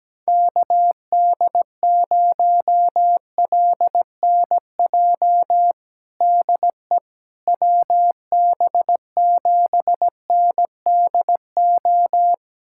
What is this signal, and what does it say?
Morse code, heard as a keyed tone, about 17 wpm, 700 Hz. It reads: KD0LNJ DE WB7NDO
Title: USA Callsign DE Callsign 17wpm